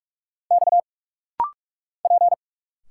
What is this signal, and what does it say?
Morse code, heard as a keyed tone, about 46 wpm, 690 Hz. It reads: X P
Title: Single Letters - Rapid-Fire 45wpm